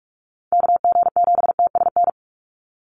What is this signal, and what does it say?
Morse code, heard as a keyed tone, about 44 wpm, 700 Hz. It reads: KG7TSN
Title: USA Call Signs 45wpm